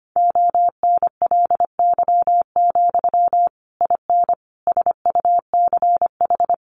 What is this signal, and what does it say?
Morse code, heard as a keyed tone, about 25 wpm, 700 Hz. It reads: ONLY, IN SUCH
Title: Sentences from Top 1500 Words 25wpm